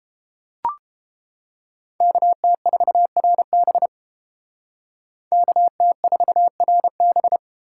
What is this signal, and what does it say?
Morse code, heard as a keyed tone, about 29 wpm, 700 Hz. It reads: KT4RB KT4RB